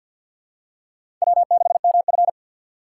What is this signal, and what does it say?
Morse code, heard as a keyed tone, about 46 wpm, 700 Hz. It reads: W6MF
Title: USA Call Signs - 4 Characters - 2x Character Spacing - 50wpm